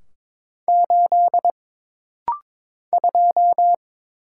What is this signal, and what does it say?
Morse code, heard as a keyed tone, about 22 wpm, 700 Hz. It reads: 8 2
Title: Single Letter-Number - Rapid-Fire 22wpm